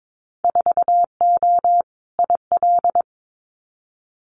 Morse code, heard as a keyed tone, about 22 wpm, 700 Hz. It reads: 4O IL
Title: Winter Field Day Exchanges Encore 22wpm